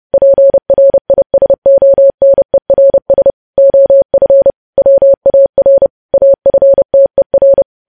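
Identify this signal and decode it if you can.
Morse code, heard as a keyed tone, about 30 wpm, 550 Hz. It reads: PRISONERS OF WAR AFTER
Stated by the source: Morse code news headlines sent at 30 wpm.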